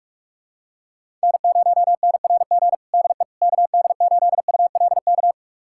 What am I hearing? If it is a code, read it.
Morse code, heard as a keyed tone, about 41 wpm, 700 Hz. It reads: N0NRG DE KD8ULK
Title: USA Callsign DE Callsign 45wpm